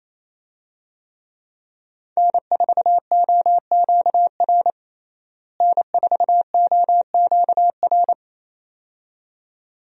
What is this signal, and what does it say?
Morse code, heard as a keyed tone, about 28 wpm, 700 Hz. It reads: N4OQR N4OQR
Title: USA Call Signs - 5 Characters - Repeated 2x - 28wpm